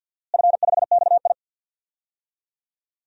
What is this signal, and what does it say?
Morse code, heard as a keyed tone, about 51 wpm, 700 Hz. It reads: F5XI
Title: DXCC-100 Call Signs - 4 Characters - Repeated 2x - 50wpm